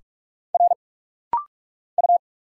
Morse code, heard as a keyed tone, about 44 wpm, 700 Hz. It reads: R U
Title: Single Letter-Number Alt - Rapid-Fire 45wpm